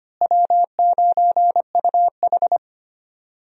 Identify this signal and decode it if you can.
Morse code, heard as a keyed tone, about 25 wpm, 700 Hz. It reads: W9UH
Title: USA Call Signs - 4 Characters - Repeated 2x - 25wpm